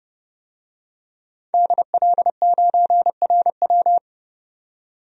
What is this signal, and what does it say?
Morse code, heard as a keyed tone, about 31 wpm, 700 Hz. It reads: DL9RW
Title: DXCC-100 Call Signs - 5 Characters - 30wpm